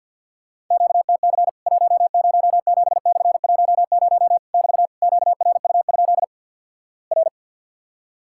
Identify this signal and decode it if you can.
Morse code, heard as a keyed tone, about 50 wpm, 700 Hz. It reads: QTC 107/10 <BT> QRU? R
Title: Worked All Europe DX Contest - QTC Exchange with Repeats - 50wpm